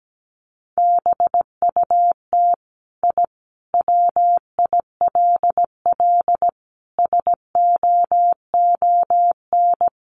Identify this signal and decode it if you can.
Morse code, heard as a keyed tone, about 17 wpm, 700 Hz. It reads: BUT I WILL SOON
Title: Sentences from Top 1300 Words 17wpm